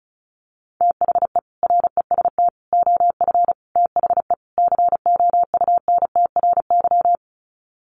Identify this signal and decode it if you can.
Morse code, heard as a keyed tone, about 34 wpm, 700 Hz. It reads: THE REST OF THE COUNTRY
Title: Sets of 5 Words 35wpm